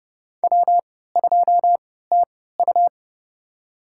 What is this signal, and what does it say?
Morse code, heard as a keyed tone, about 30 wpm, spaced as for 17 wpm, 700 Hz. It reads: W2TU
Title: USA Call Signs - 3x Character Spacing - 30wpm